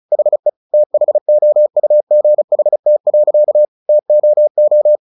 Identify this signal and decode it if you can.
Morse code, heard as a keyed tone, about 34 wpm, 600 Hz. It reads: HE THOUGHT. TOO
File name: the old man and the sea 2 35wpm